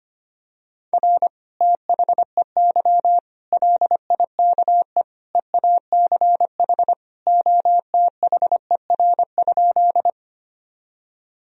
Morse code, heard as a keyed tone, about 25 wpm, 700 Hz. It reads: R THEY LIKE EACH OTHER?
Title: Sentences from Top 200 Words and Abbreviations 25wpm